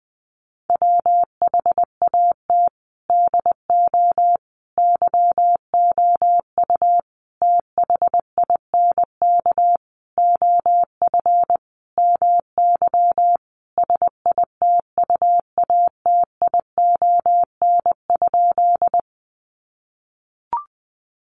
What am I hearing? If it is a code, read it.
Morse code, heard as a keyed tone, about 20 wpm, 700 Hz. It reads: WHAT DO YOU THINK OF MY SITUATION?
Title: Sentences from Top 1200 Words 20wpm